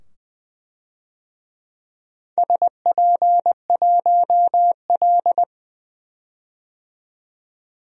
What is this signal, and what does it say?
Morse code, heard as a keyed tone, about 20 wpm, 700 Hz. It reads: SP1L
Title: DXCC-100 Call Signs - 4 Characters - 20wpm